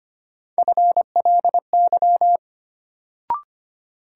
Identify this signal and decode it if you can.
Morse code, heard as a keyed tone, about 25 wpm, 700 Hz. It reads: FLY
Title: Sentences from Top 500 Words - New Word Drill - 25wpm